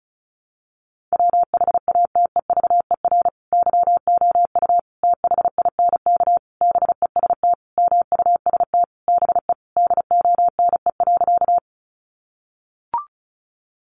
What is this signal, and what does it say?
Morse code, heard as a keyed tone, about 34 wpm, 700 Hz. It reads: WHATEVER YOU THINK BEST MUST BE DONE.
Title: Sentences from Top 1100 Words 35wpm